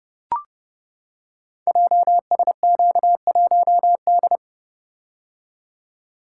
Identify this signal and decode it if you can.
Morse code, heard as a keyed tone, about 30 wpm, 700 Hz. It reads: JSQ1D
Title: Code Groups - 5 Letter:Numbers - 30wpm